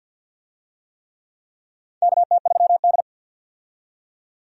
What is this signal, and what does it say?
Morse code, heard as a keyed tone, about 50 wpm, 700 Hz. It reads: KT3D
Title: USA Call Signs - 4 Characters - 50wpm